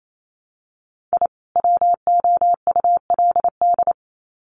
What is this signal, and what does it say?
Morse code, heard as a keyed tone, about 28 wpm, 700 Hz. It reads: I WOULD
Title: Sets of 2 Words 28wpm